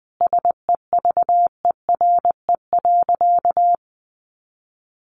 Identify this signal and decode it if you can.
Morse code, heard as a keyed tone, about 19 wpm, 700 Hz. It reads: SEVERE.
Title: Sentences from Top 1400 Words 20wpm